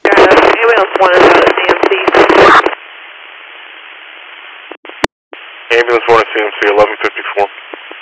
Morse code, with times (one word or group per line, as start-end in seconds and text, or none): none